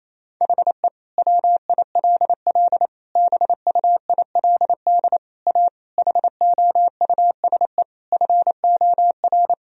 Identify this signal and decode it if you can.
Morse code, heard as a keyed tone, about 27 wpm, 700 Hz. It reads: HE WILL BUILD A HOUSE FOR
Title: Sentences from Top 1000 Words - Review Part 1 - 28wpm